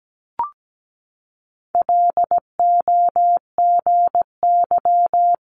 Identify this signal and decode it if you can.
Morse code, heard as a keyed tone, about 17 wpm, 690 Hz. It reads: LOGY
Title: Multi-letter Phonemes Repeated 3x 17wpm